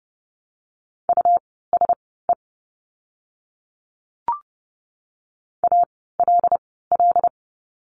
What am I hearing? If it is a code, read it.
Morse code, heard as a keyed tone, about 30 wpm, spaced as for 17 wpm, 700 Hz. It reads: USE ALL
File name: course / 30:10wpm / 237 Lesson 38f - Letter q - Words 30wpm